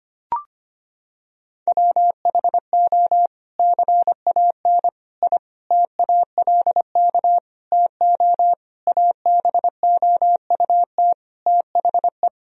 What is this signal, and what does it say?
Morse code, heard as a keyed tone, about 25 wpm, 700 Hz. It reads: WHO CAN I TALK TO ABOUT THE
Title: Sentences from Top 1000 Words - Review Part 4 - 25wpm